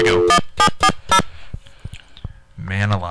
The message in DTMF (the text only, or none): 1740